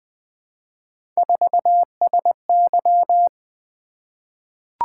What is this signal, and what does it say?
Morse code, heard as a keyed tone, about 20 wpm, 700 Hz. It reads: 4SY
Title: Three Letter-Number Combinations - 20wpm